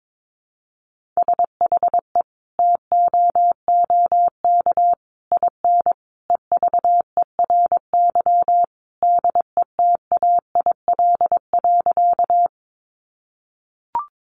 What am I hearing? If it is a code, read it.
Morse code, heard as a keyed tone, about 22 wpm, 700 Hz. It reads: SHE TOOK IN EVERY DETAIL.
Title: Sentences from Top 1500 Words 22wpm